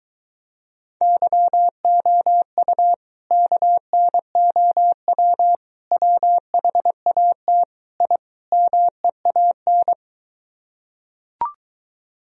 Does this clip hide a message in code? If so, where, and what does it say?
Yes: Morse code from 1.0 to 9.9 s, heard as a keyed tone, about 23 wpm, 700 Hz. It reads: YOU KNOW WHAT I MEAN